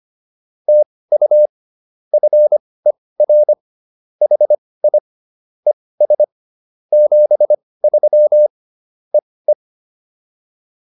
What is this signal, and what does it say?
Morse code, heard as a keyed tone, about 25 wpm, spaced as for 18 wpm, 600 Hz. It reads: TU FER HI ES 73 EE
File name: POTA208 18